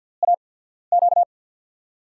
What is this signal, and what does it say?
Morse code, heard as a keyed tone, about 48 wpm, spaced as for 14 wpm, 700 Hz. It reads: AQ
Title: USA Call Signs - 6 Characters - 8x Character Spacing - 50wpm